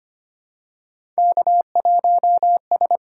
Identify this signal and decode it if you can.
Morse code, heard as a keyed tone, about 25 wpm, 700 Hz. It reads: K1S
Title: USA Call Signs - 5 Characters - 6x Character Spacing - 25wpm